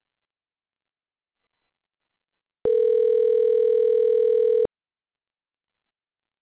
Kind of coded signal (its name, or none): telephone call progress tone